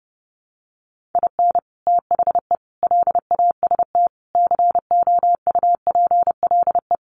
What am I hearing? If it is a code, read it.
Morse code, heard as a keyed tone, about 30 wpm, 700 Hz. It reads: IN THE LAST COUPLE
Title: Sets of 5 Words 30wpm